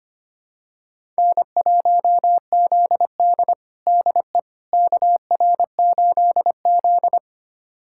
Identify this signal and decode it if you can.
Morse code, heard as a keyed tone, about 25 wpm, 700 Hz. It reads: N1ZD DE KR8Z
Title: USA Callsign DE Callsign - 4 Characters - 25wpm